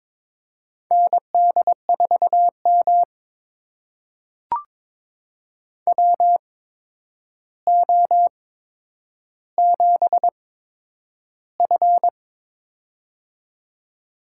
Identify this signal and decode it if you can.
Morse code, heard as a keyed tone, about 22 wpm, spaced as for 9 wpm, 700 Hz. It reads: ND4M WO7F
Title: USA Call Signs - 4 Characters - 8x Character Spacing - 22wpm